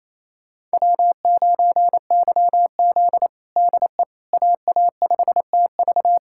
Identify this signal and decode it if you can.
Morse code, heard as a keyed tone, about 28 wpm, 700 Hz. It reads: W9YZ DE AA5TV
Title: USA Callsign DE Callsign 28wpm